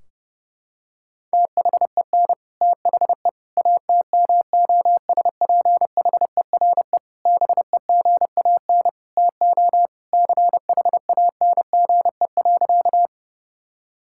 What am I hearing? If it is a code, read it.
Morse code, heard as a keyed tone, about 32 wpm, 700 Hz. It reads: THEN THE ATMOSPHERE BEGAN TO CHANGE.
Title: Sentences from Top 1600 Words 30wpm